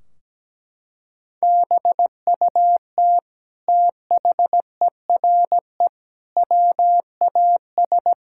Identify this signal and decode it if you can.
Morse code, heard as a keyed tone, about 17 wpm, 700 Hz. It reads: BUT THERE WAS
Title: Sentences from Top 1600 Words 17wpm